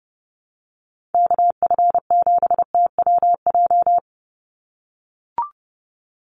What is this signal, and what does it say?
Morse code, heard as a keyed tone, about 30 wpm, 700 Hz. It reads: KF7TWJ
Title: USA Call Signs 30wpm